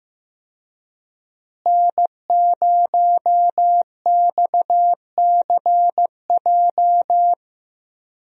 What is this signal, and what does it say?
Morse code, heard as a keyed tone, about 15 wpm, 700 Hz. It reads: N0XCJ
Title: USA Call Signs - 5 Characters - Repeated 3x - 15wpm